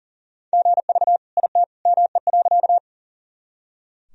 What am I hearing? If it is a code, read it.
Morse code, heard as a keyed tone, about 40 wpm, 700 Hz. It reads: GV IT ME.
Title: Sentences from Top 200 Words and Abbreviations 40wpm